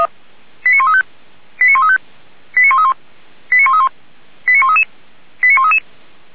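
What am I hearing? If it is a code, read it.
Five-tone selective call (ZVEI), heard as quick runs of five tones, tones of about 80 ms. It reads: 89135 89135 89131 89131 89133 89133
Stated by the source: ZVEI five-tone selective call received on an SDR.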